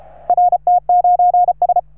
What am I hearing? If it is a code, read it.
Morse code, heard as a keyed tone, about 33 wpm, 710 Hz. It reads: RT9S
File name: RS3A-20